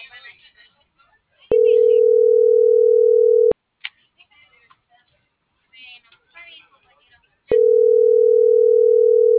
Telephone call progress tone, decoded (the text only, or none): ringback tone